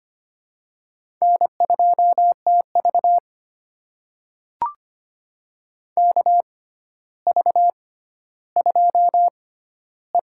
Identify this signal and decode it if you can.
Morse code, heard as a keyed tone, about 25 wpm, spaced as for 13 wpm, 700 Hz. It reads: N2TV KV2E
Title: USA Call Signs - 4 Characters - 6x Character Spacing - 25wpm